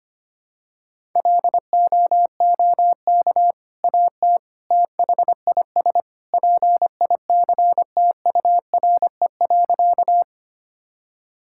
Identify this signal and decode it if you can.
Morse code, heard as a keyed tone, about 25 wpm, 700 Hz. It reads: LOOK AT THIS PICTURE.
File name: practice / 25wpm / Sentences with 2-Word Phrases - 25wpm